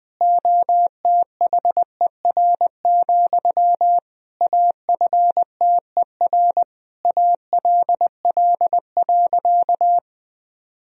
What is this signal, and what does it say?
Morse code, heard as a keyed tone, about 20 wpm, 700 Hz. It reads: OTHER, AFTER ALL.
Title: Sentences from Top 1000 Words - Review Part 1 - 20wpm